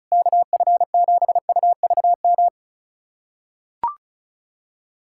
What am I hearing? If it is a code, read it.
Morse code, heard as a keyed tone, about 36 wpm, 690 Hz. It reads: KF7UVM
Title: USA Callsign DE Callsign 35wpm